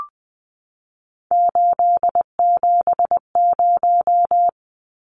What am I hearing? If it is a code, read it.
Morse code, heard as a keyed tone, about 20 wpm, 700 Hz. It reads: 870
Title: Numbers - 3-Digit 20wpm